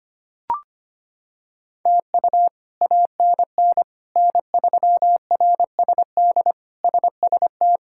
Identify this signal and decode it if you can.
Morse code, heard as a keyed tone, about 25 wpm, 700 Hz. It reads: TU ANN N3RSD SST
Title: K1USN SST QSO Practice - 25wpm